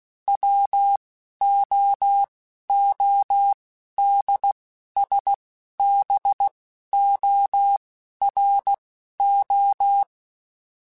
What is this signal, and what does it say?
Morse code, heard as a keyed tone, about 16 wpm, spaced as for 11 wpm, 800 Hz. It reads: WOODSBORO